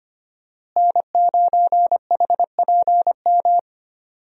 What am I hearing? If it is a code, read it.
Morse code, heard as a keyed tone, about 25 wpm, 700 Hz. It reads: N9HPM
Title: USA Call Signs - 5 Characters - Repeated 3x - 25wpm